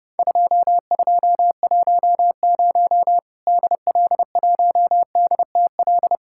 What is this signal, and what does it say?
Morse code, heard as a keyed tone, about 30 wpm, 700 Hz. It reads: 2210 DL1DTL
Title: Worked All Europe DX Contest - QTC Exchange - 30wpm